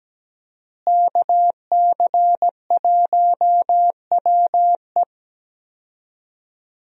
Morse code, heard as a keyed tone, about 17 wpm, 700 Hz. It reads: KC1WE
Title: USA Call Signs - 5 Characters - Repeated 2x - 17wpm